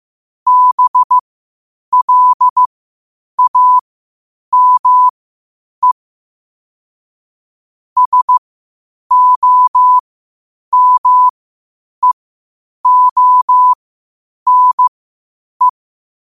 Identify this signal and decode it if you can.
Morse code, heard as a keyed tone, about 15 wpm, spaced as for 8 wpm, 1000 Hz. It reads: BLAME SOMEONE
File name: qotdfiles / 8wpm